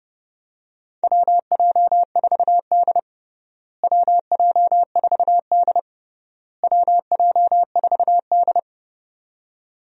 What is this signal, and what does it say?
Morse code, heard as a keyed tone, about 30 wpm, 700 Hz. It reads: WJ4D WJ4D WJ4D
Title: USA Call Signs - 4 Characters - Repeated 3x - 30wpm